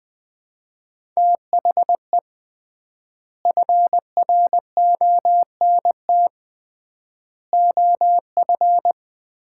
Morse code, heard as a keyed tone, about 21 wpm, 700 Hz. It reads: THE FRONT OF
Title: 3x Word-Spacing - Sets of 4 Words - 20wpm